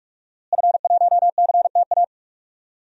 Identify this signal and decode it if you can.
Morse code, heard as a keyed tone, about 44 wpm, 700 Hz. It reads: F1CTA
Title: DXCC-100 Call Signs - 5 Characters - Repeated 2x - 45wpm